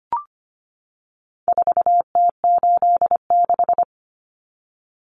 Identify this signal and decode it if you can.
Morse code, heard as a keyed tone, about 25 wpm, 700 Hz. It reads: 4T86
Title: Common Cut Numbers - 4-Digit 25wpm